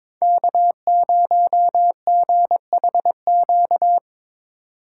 Morse code, heard as a keyed tone, about 22 wpm, 700 Hz. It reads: K0GHQ
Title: USA Call Signs - 5 Characters - 6x Character Spacing - 22wpm